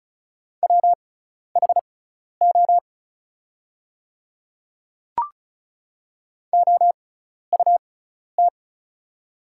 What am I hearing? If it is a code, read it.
Morse code, heard as a keyed tone, about 36 wpm, spaced as for 12 wpm, 700 Hz. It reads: WHO OUT
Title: 6x Character Spacing - Top 300 Words - 35wpm